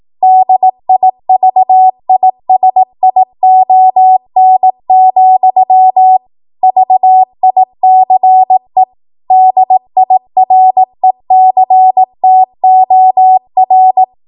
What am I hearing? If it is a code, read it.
Morse code, heard as a keyed tone, about 18 wpm, 750 Hz. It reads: DIVISION, VICE DIRECTOR